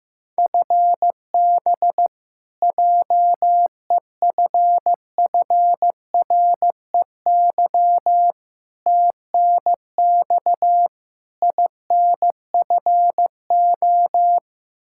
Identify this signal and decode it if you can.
Morse code, heard as a keyed tone, about 15 wpm, 700 Hz. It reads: FB JEFFREY TNX INFO
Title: Practice QSO - Exchange 3 - More Info - 15wpm